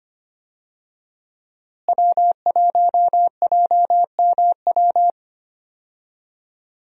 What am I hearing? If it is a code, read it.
Morse code, heard as a keyed tone, about 25 wpm, 700 Hz. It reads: W1JMW
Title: USA Call Signs - 5 Characters - Repeated 3x - 25wpm